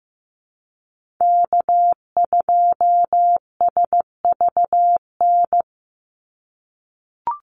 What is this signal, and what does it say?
Morse code, heard as a keyed tone, about 15 wpm, 700 Hz. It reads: K2SVN
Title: USA Call Signs 15wpm